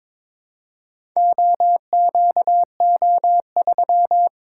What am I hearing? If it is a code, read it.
Morse code, heard as a keyed tone, about 22 wpm, 700 Hz. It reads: OQO3
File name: practice / 22wpm / Four Letter-Number Combinations - 22wpm